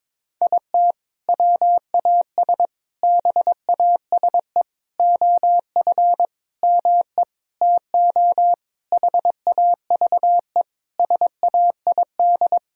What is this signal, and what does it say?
Morse code, heard as a keyed tone, about 22 wpm, 700 Hz. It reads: IT WAS BASE OF ME TO HAVE SAID
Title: Sentences from Top 1000 Words - Review Part 1 - 22wpm